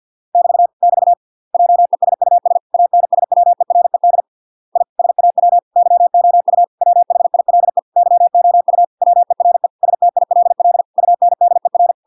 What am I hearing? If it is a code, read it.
Morse code, heard as a keyed tone, about 50 wpm, 700 Hz. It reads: <BT> <BT> JESUS ANSWERED I SAW YOU WHILE YOU WERE STILL UNDER